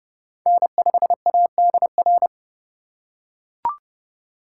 Morse code, heard as a keyed tone, about 31 wpm, 700 Hz. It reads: N5ADR
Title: USA Call Signs - 2x Character Spacing - 30wpm